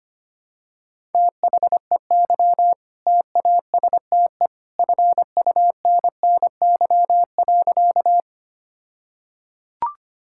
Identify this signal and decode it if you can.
Morse code, heard as a keyed tone, about 25 wpm, 700 Hz. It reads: THEY TASTE FUNNY.
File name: practice / 25wpm / All-Age Appropriate Jokes - 25wpm